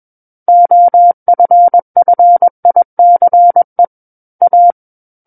Morse code, heard as a keyed tone, about 21 wpm, 700 Hz. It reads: OFFICE A
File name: Book One / 12-Chapter 12 - 21 WPM